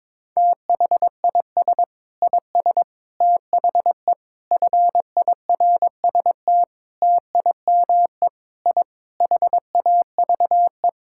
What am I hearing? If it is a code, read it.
Morse code, heard as a keyed tone, about 22 wpm, 700 Hz. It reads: THIS IS THE FIRST TIME I HAVE
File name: practice / 22wpm / Sentences with 4-Word Phrases - 22wpm